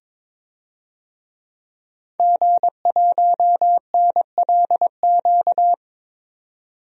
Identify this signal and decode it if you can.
Morse code, heard as a keyed tone, about 22 wpm, 700 Hz. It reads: G1NLQ
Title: DXCC-100 Call Signs - 5 Characters - 22wpm